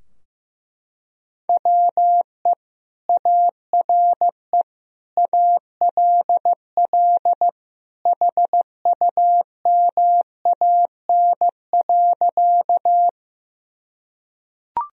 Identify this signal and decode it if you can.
Morse code, heard as a keyed tone, about 15 wpm, 700 Hz. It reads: WE ARE ALL HUMAN.
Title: Sentences from Top 1000 Words - Review Part 3 - 15wpm